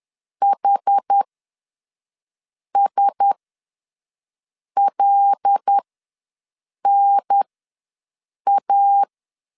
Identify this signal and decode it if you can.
Morse code, heard as a keyed tone, about 10 wpm, spaced as for 5 wpm, 800 Hz. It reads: HSLNA
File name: LEZIONI / 9 2 ws 10wpm 4130